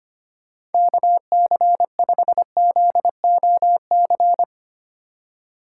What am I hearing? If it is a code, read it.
Morse code, heard as a keyed tone, about 32 wpm, 700 Hz. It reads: KC5ZOC